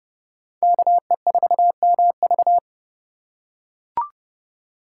Morse code, heard as a keyed tone, about 28 wpm, 700 Hz. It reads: KE4MV